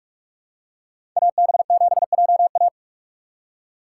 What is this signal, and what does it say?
Morse code, heard as a keyed tone, about 46 wpm, 700 Hz. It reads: AB7JA
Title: USA Call Signs - 3x Character Spacing - 45wpm